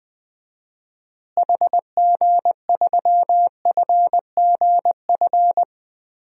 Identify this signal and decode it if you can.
Morse code, heard as a keyed tone, about 20 wpm, 700 Hz. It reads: HG3FGF
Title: DXCC-100 Call Signs - 6 Characters - Repeated 3x - 20wpm